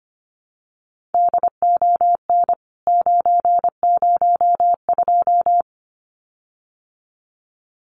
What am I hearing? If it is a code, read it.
Morse code, heard as a keyed tone, about 25 wpm, 700 Hz. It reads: DON 902
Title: CWT Exchange 25wpm